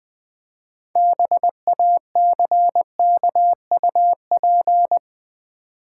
Morse code, heard as a keyed tone, about 20 wpm, 700 Hz. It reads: BACKUP
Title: 6 Letter Words - First 3 Letters Given - 20wpm